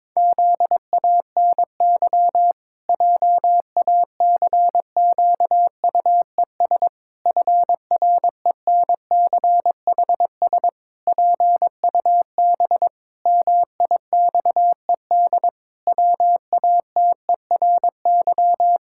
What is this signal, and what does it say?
Morse code, heard as a keyed tone, about 22 wpm, 700 Hz. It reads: ZANY JACQUES FRENCHS PUB MIXED WATERY
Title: Pangram Challenge v2 - 22wpm